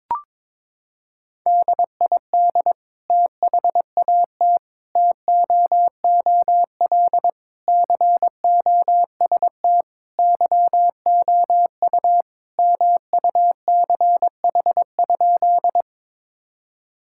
Morse code, heard as a keyed tone, about 22 wpm, 700 Hz. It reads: DID THAT TOOL COST YOU MUCH?
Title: Sentences from Top 1000 Words - Review Part 6 - 22wpm